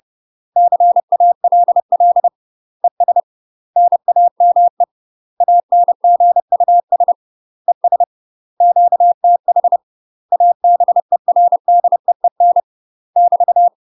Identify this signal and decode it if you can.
Morse code, heard as a keyed tone, about 30 wpm, 700 Hz. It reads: CALL ES NAME ANGUS ES QTH ABERDEEN <BT>
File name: UK MASTER QSOs TRAINING / 30WPM / UK MASTER QSOs 0000